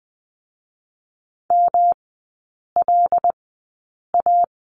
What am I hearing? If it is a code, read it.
Morse code, heard as a keyed tone, about 20 wpm, spaced as for 8 wpm, 700 Hz. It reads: MLA
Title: 127 Lesson 21e - Number 6 - Three Characters 20wpm